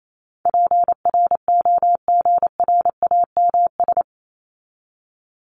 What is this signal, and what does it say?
Morse code, heard as a keyed tone, about 28 wpm, 700 Hz. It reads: PROGRAMS
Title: ES:S Suffix - Top 100 Words with Base Words 28wpm